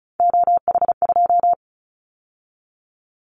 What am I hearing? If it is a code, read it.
Morse code, heard as a keyed tone, about 36 wpm, 700 Hz. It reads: OH2
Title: Callsign Prefixes for Top 50 DXCC - 35wpm